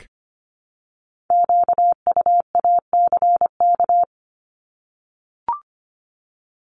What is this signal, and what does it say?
Morse code, heard as a keyed tone, about 25 wpm, 700 Hz. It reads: QUACK
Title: Multi-letter Phonemes Repeated 3x 25wpm